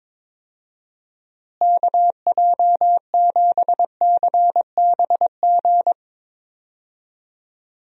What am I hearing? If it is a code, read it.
Morse code, heard as a keyed tone, about 22 wpm, 700 Hz. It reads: KJ7CBG
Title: USA Call Signs - 6 Characters - Repeated 2x - 22wpm